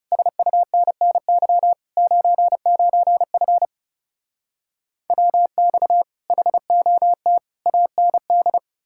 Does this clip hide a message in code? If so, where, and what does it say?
Yes: Morse code from 0.1 to 8.6 s, heard as a keyed tone, about 40 wpm, 700 Hz. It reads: SUNNY 99F WX HOT AND